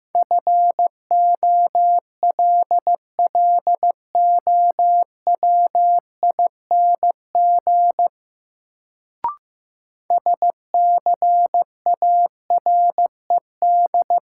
Morse code, heard as a keyed tone, about 15 wpm, 700 Hz. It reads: FOLLOWING SCARED
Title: Sentences from Top 700 Words - New Word Drill - 15wpm